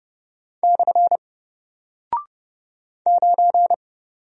Morse code, heard as a keyed tone, about 30 wpm, 700 Hz. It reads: / 9
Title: Single Character (Letter:Number:Punctuation:Prosign) Alt 30wpm